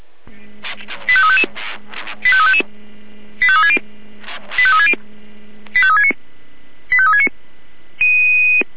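Five-tone selective call (ZVEI), heard as quick runs of five tones, tones of about 70 ms. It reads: 85355 85360 85360 85369 85369
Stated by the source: ZVEI five-tone selective call received on an SDR.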